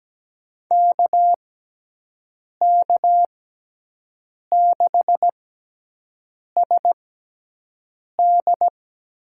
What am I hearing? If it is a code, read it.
Morse code, heard as a keyed tone, about 17 wpm, spaced as for 6 wpm, 700 Hz. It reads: KK6SD
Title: USA Call Signs - 6x Character Spacing - 17wpm